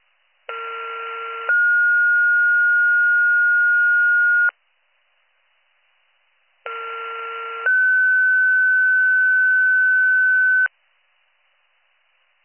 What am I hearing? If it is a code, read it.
Two-tone page, heard as a one-second tone, then a three-second tone, about 1370 then 1430 Hz, about 1370 then 1500 Hz. It is heard from 0.5 to 10.7 s.